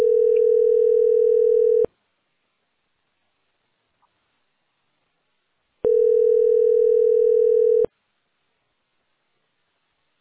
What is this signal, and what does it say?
Ringback tone (North America), 440 + 480 Hz.